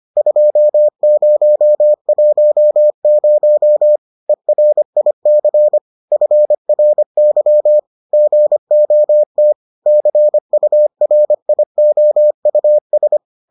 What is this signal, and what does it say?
Morse code, heard as a keyed tone, about 25 wpm, 600 Hz. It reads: 2010 ERIC FRY GOT CURIOUS